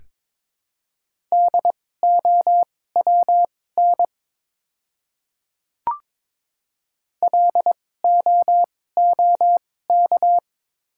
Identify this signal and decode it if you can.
Morse code, heard as a keyed tone, about 23 wpm, spaced as for 16 wpm, 700 Hz. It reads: DOWN LOOK
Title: 2x Character Spacing - Top 100 Words - 22wpm